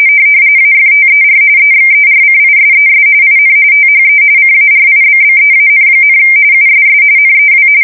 RTTY, about 45.45 baud. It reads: AFTER EXERCISES IN THE CARIBBEAN, USS NEW
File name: fsk170 / battleshipnj